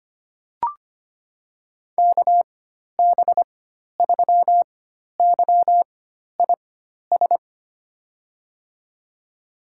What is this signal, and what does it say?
Morse code, heard as a keyed tone, about 25 wpm, spaced as for 12 wpm, 700 Hz. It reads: KB3YIS
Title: USA Call Signs - 6 Characters - 4x Character Spacing - 25wpm